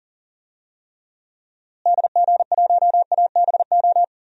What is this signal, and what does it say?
Morse code, heard as a keyed tone, about 37 wpm, 700 Hz. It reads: DG1ABO
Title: DXCC-100 Call Signs - 6 Characters - Repeated 3x - 40wpm